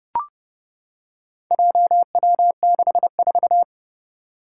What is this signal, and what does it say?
Morse code, heard as a keyed tone, about 30 wpm, 700 Hz. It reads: JW64
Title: Four Challenging Letter-Number Combinations 30wpm